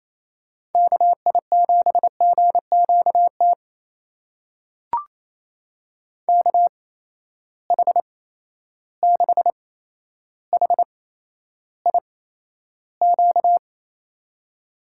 Morse code, heard as a keyed tone, about 28 wpm, spaced as for 12 wpm, 700 Hz. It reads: KI7GQT KH6HIQ
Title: USA Call Signs - 6 Characters - 8x Character Spacing - 28wpm